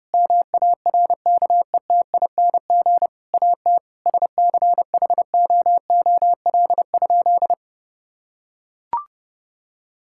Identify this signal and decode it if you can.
Morse code, heard as a keyed tone, about 30 wpm, 700 Hz. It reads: MARKETING AT SCHOOL?
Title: Sentences from Top 1000 Words - Review Part 5 - 30wpm